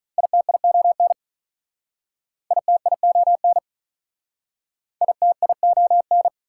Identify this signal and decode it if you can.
Morse code, heard as a keyed tone, about 43 wpm, 700 Hz. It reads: ITION ITION ITION
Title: Speed-Racing Prefixes and Suffixes 35wpm